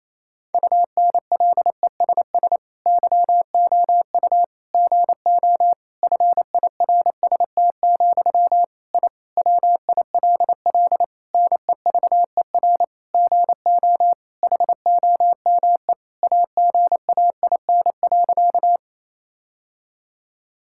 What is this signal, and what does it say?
Morse code, heard as a keyed tone, about 28 wpm, 700 Hz. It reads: UNLESS YOU GO FIRST, I WILL NEVER GO HOME AGAIN.
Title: Sentences from Top 1000 Words - Review Part 6 - 28wpm